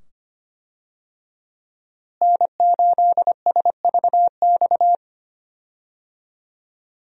Morse code, heard as a keyed tone, about 25 wpm, 700 Hz. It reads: N8SVX
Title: USA Call Signs - 5 Characters - Repeated 2x - 25wpm